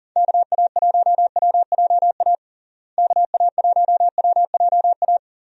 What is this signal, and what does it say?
Morse code, heard as a keyed tone, about 40 wpm, 700 Hz. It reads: KA1WJA KA1WJA
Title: USA Call Signs - 6 Characters - Repeated 3x - 40wpm